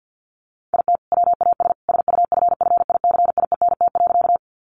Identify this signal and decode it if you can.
Morse code, heard as a keyed tone, about 50 wpm, 700 Hz. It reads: IT WAS SUFFICIENT.
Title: Sentences from Top 1400 Words 50wpm